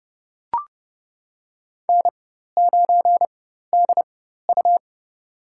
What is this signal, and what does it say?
Morse code, heard as a keyed tone, about 30 wpm, spaced as for 14 wpm, 700 Hz. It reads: N9DU
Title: USA Call Signs - 4 Characters - 4x Character Spacing - 30wpm